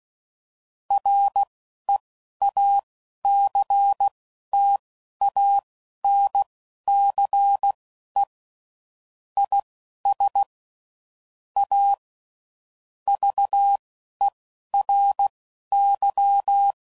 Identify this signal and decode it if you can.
Morse code, heard as a keyed tone, about 16 wpm, spaced as for 11 wpm, 800 Hz. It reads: REACTANCE IS A VERY
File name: CW Code Groups @10WPM / Technical Subjects / Reactance